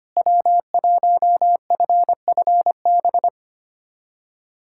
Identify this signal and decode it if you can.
Morse code, heard as a keyed tone, about 25 wpm, 700 Hz. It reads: W1FFB
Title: USA Callsign DE Callsign - 5 Characters - 25wpm